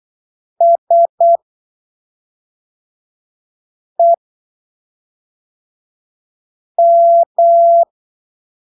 Morse code, heard as a keyed tone, about 8 wpm, spaced as for 3 wpm, 680 Hz. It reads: SEM
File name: arquivos / texto / texto 21 08ppm